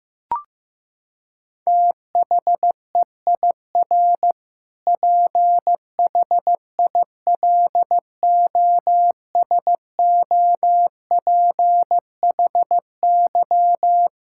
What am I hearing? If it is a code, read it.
Morse code, heard as a keyed tone, about 15 wpm, 700 Hz. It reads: THEIR PHILOSOPHY
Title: Sentences from Top 1000 Words - Review Part 6 - 15wpm